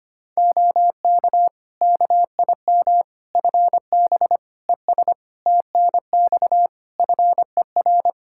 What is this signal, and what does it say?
Morse code, heard as a keyed tone, about 25 wpm, 700 Hz. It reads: OK KIM FB ES TNX FER
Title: Practice QSO - Exchange 2 - Info - 25wpm